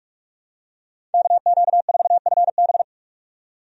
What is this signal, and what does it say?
Morse code, heard as a keyed tone, about 46 wpm, 700 Hz. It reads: KQ4FB
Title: USA Call Signs - 5 Characters - 45wpm